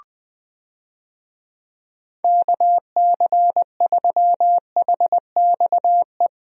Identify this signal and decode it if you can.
Morse code, heard as a keyed tone, about 20 wpm, 700 Hz. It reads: KC3HXE
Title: USA Call Signs - 6 Characters - Repeated 3x - 20wpm